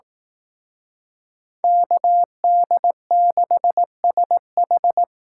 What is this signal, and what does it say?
Morse code, heard as a keyed tone, about 18 wpm, 700 Hz. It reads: KD6SH
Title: Speed Racing - USA 2x2 Call Signs 15wpm